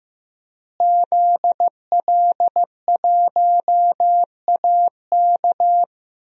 Morse code, heard as a keyed tone, about 15 wpm, 700 Hz. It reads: ZL1AK
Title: DXCC-100 Call Signs - 5 Characters - Repeated 3x - 15wpm